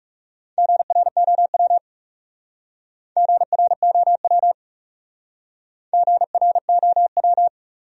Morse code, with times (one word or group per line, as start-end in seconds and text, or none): 0.6-1.8 GROW
3.2-4.5 GROW
5.9-7.5 GROW